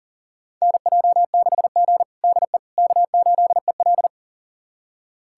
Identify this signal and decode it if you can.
Morse code, heard as a keyed tone, about 40 wpm, 700 Hz. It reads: NJ6G DE K8EL
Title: USA Callsign DE Callsign - 4 Characters - 40wpm